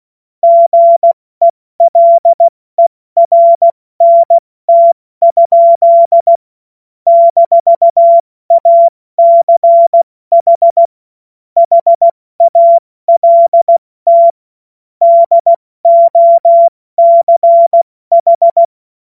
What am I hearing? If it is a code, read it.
Morse code, heard as a keyed tone, about 16 wpm, 680 Hz. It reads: GELERNT? - ACH HALT DOCH
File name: Morsekurs / Witze / Witz 24. August 2021, 70 F80